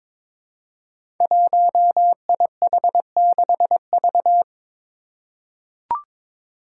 Morse code, heard as a keyed tone, about 22 wpm, 700 Hz. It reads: 1IH6V
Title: Five Challenging Letter-Number Combinations 22wpm